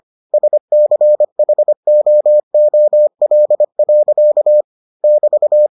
Morse code, heard as a keyed tone, about 25 wpm, 600 Hz. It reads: SCHOOL. <BT>